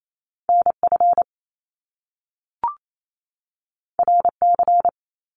Morse code, heard as a keyed tone, about 28 wpm, 700 Hz. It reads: NF RC
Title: Top Two Letter Combinations in Words 28wpm